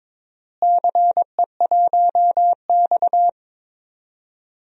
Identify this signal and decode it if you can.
Morse code, heard as a keyed tone, about 22 wpm, 700 Hz. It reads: CE1X
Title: DXCC-100 Call Signs - 4 Characters - Repeated 2x - 22wpm